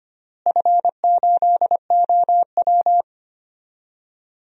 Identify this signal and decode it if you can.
Morse code, heard as a keyed tone, about 25 wpm, 700 Hz. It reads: F8OW
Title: DXCC-100 Call Signs - 4 Characters - Repeated 3x - 25wpm